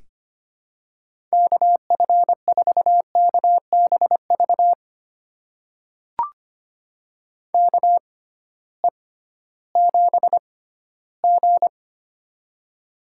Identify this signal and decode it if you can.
Morse code, heard as a keyed tone, about 25 wpm, spaced as for 15 wpm, 700 Hz. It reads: KF4KBV KE7G
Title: USA Call Signs - 6x Character Spacing - 25wpm